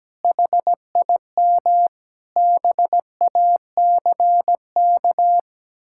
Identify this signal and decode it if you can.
Morse code, heard as a keyed tone, about 17 wpm, 700 Hz. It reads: HIM BACK
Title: Sentences from Top 1000 Words - Review Part 6 - 17wpm